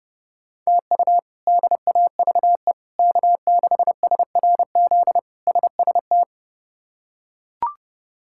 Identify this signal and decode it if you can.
Morse code, heard as a keyed tone, about 32 wpm, 700 Hz. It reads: TU DAVE K6SRZ SST
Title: K1USN SST QSO Practice - 30wpm